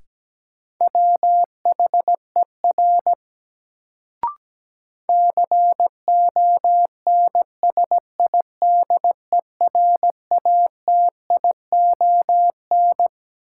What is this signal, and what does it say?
Morse code, heard as a keyed tone, about 17 wpm, 700 Hz. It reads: WHER CONSIDERATION
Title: Sentences from Top 1500 Words - New Word Drill - 17wpm